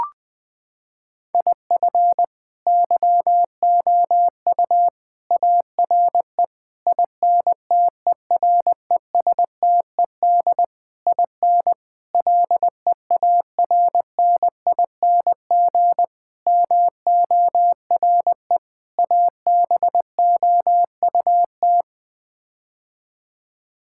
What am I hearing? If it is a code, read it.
Morse code, heard as a keyed tone, about 20 wpm, 700 Hz. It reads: IF YOU ARE INTERESTED IN LEARNING MORE ABOUT
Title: Sets of 8 Words - 20wpm